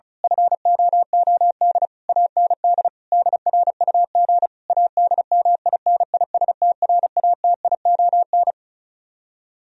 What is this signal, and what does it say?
Morse code, heard as a keyed tone, about 32 wpm, 700 Hz. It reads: FOOD AND DRUG ADMINISTRATION
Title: Speed-Racing - Sets of 5 Words 35wpm